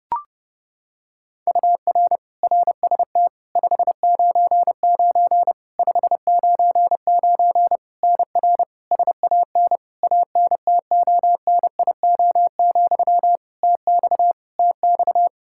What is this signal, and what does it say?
Morse code, heard as a keyed tone, about 30 wpm, 700 Hz. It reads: UR RST 599 599 NR SAN ANTONIO, TX TX
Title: K1USN SST QSO Practice 30wpm